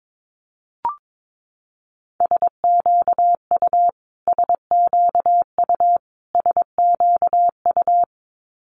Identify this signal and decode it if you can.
Morse code, heard as a keyed tone, about 22 wpm, 700 Hz. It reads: SQU SQU SQU
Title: Multi-letter Phonemes Repeated 3x 22wpm